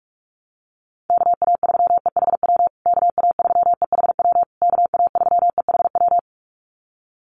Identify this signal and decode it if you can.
Morse code, heard as a keyed tone, about 44 wpm, 700 Hz. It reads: KA3EHW KA3EHW KA3EHW
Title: USA Call Signs Repeated 3x 45wpm